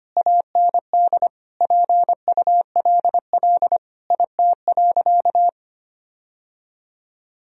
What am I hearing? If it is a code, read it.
Morse code, heard as a keyed tone, about 25 wpm, 700 Hz. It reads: AND PULL IT.
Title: Sentences from Top 1000 Words - Review Part 2 - 25wpm